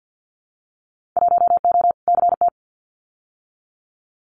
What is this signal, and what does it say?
Morse code, heard as a keyed tone, about 50 wpm, 700 Hz. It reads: 1O CT
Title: Winter Field Day Exchanges 50wpm